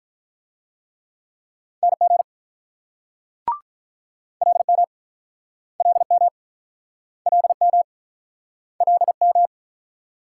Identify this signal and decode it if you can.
Morse code, heard as a keyed tone, about 46 wpm, 700 Hz. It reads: NG LM LM LM LM